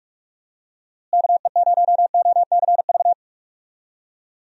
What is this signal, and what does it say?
Morse code, heard as a keyed tone, about 45 wpm, 700 Hz. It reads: KE0OCV
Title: USA Call Signs - 6 Characters - Repeated 2x - 45wpm